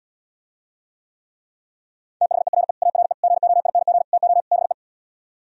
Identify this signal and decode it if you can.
Morse code, heard as a keyed tone, about 77 wpm, 700 Hz. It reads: THERE ARE GOING TO BE